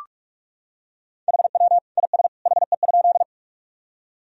Morse code, heard as a keyed tone, about 46 wpm, 700 Hz. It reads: HW IS HE?
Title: Sentences from Top 100 Words and Abbreviations 45wpm